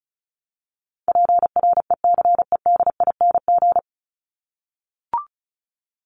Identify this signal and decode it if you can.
Morse code, heard as a keyed tone, about 34 wpm, 700 Hz. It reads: PRECEDING
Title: ING Suffix - Top 100 Words with Base Words 35wpm